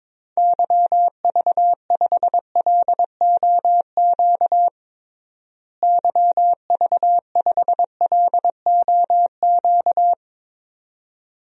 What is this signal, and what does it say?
Morse code, heard as a keyed tone, about 22 wpm, 700 Hz. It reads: YV5LOQ YV5LOQ
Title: DXCC-100 Call Signs - 6 Characters - Repeated 2x - 22wpm